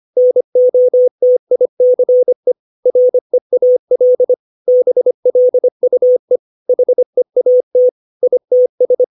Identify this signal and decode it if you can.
Morse code, heard as a keyed tone, about 25 wpm, 500 Hz. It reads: NOTICE REAL BLUE HEAT ITS
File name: 632 words / 632 Words 25WPM